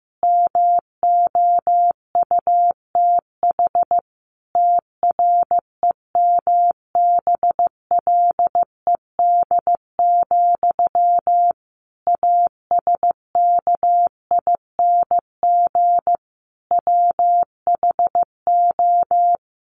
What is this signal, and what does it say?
Morse code, heard as a keyed tone, about 15 wpm, 700 Hz. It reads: MOUTH TREMBLED, ASKING WHO
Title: A Sound Of Thunder 15wpm